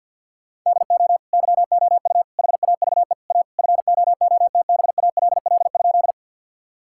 Morse code, heard as a keyed tone, about 50 wpm, 700 Hz. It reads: DO YOU HAVE A FOOTBALL?
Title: Sentences from Top 1600 Words 50wpm